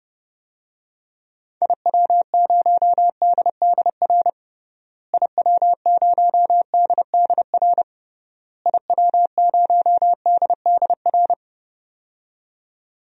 Morse code, heard as a keyed tone, about 32 wpm, 700 Hz. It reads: IW0DDR IW0DDR IW0DDR
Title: DXCC-100 Call Signs - 6 Characters - Repeated 3x - 30wpm